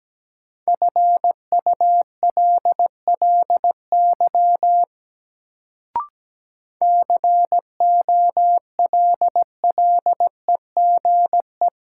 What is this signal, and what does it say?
Morse code, heard as a keyed tone, about 17 wpm, 700 Hz. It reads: FULLY COLLEGE
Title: Sentences from Top 1300 Words - New Word Drill Alt - 17wpm